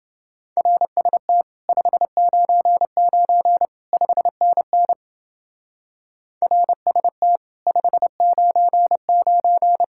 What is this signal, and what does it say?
Morse code, heard as a keyed tone, about 30 wpm, 700 Hz. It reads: RST 599 5NN RST 599